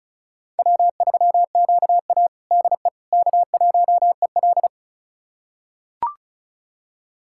Morse code, heard as a keyed tone, about 36 wpm, 690 Hz. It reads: W3QA DE K1EL
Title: USA Callsign DE Callsign - 4 Characters - 35wpm